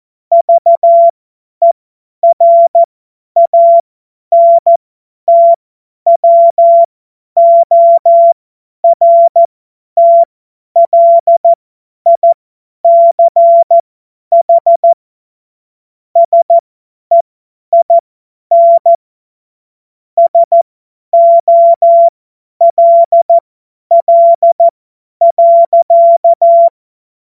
Morse code, heard as a keyed tone, about 14 wpm, spaced as for 10 wpm, 680 Hz. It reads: VERANTWORTLICH SEIN SOLL.
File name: Morsekurs / Witze / Witz 12 Dezember 2023 50 F70